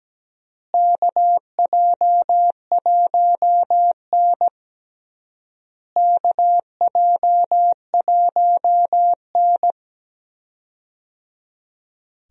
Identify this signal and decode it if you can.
Morse code, heard as a keyed tone, about 17 wpm, 700 Hz. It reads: KJ1N KJ1N
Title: USA Call Signs - 4 Characters - Repeated 3x - 17wpm